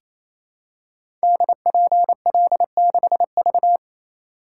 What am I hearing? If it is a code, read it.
Morse code, heard as a keyed tone, about 27 wpm, 700 Hz. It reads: DPL6V
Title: Code Groups - 5 Letter:Numbers - 28wpm